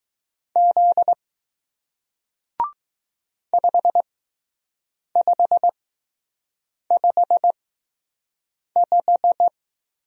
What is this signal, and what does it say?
Morse code, heard as a keyed tone, about 20 wpm, spaced as for 6 wpm, 700 Hz. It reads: Z 5555